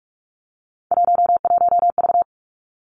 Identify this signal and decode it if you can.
Morse code, heard as a keyed tone, about 46 wpm, 700 Hz. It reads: 11V
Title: Three Challenging Letter-Number Combinations 45wpm